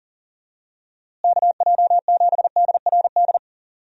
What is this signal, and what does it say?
Morse code, heard as a keyed tone, about 40 wpm, 700 Hz. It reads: KJ7DRD
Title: USA Call Signs - 6 Characters - Repeated 2x - 40wpm